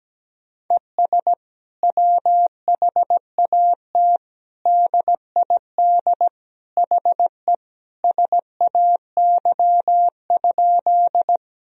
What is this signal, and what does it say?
Morse code, heard as a keyed tone, about 17 wpm, 700 Hz. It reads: ES WHAT DID HE SAY?
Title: Sentences from Top 200 Words and Abbreviations 17wpm